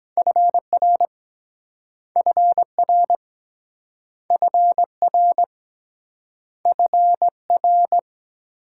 Morse code, heard as a keyed tone, about 23 wpm, 700 Hz. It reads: FR FR FR FR